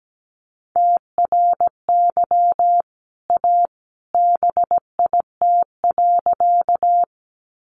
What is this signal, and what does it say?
Morse code, heard as a keyed tone, about 17 wpm, 700 Hz. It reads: TRY A BIT.
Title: Sentences from Top 900 Words 17wpm